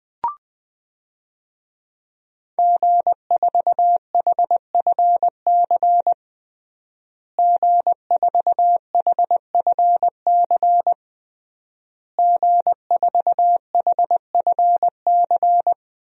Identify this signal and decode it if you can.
Morse code, heard as a keyed tone, about 20 wpm, 700 Hz. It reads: G4HFC G4HFC G4HFC
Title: DXCC-100 Call Signs - 5 Characters - Repeated 3x - 20wpm